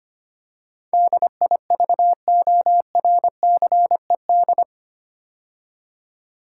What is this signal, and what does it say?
Morse code, heard as a keyed tone, about 25 wpm, 700 Hz. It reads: DIVORCED
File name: 8 Letter Words - First 3 Letters Given - 25wpm